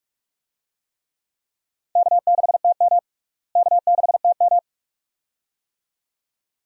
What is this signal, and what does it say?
Morse code, heard as a keyed tone, about 46 wpm, 700 Hz. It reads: K6TM K6TM
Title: USA Call Signs - 4 Characters - Repeated 2x - 45wpm